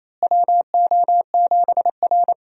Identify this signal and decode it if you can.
Morse code, heard as a keyed tone, about 27 wpm, 700 Hz. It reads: WO7R
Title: USA Call Signs - 4 Characters - Repeated 3x - 28wpm